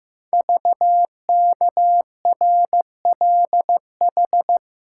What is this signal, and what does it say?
Morse code, heard as a keyed tone, about 15 wpm, 700 Hz. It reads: VKRLH
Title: Code Groups - 5 Letters - 15wpm